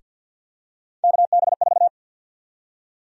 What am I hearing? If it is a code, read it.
Morse code, heard as a keyed tone, about 50 wpm, 700 Hz. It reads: KB4
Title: USA Call Sign Prefixes - Repeated 2x - 50wpm